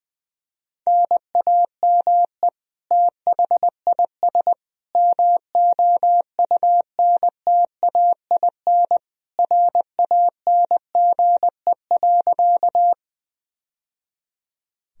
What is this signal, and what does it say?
Morse code, heard as a keyed tone, about 20 wpm, 700 Hz. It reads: NAME THIS MOUNTAIN RANGE.
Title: Sentences from Top 1000 Words - Review Part 4 - 20wpm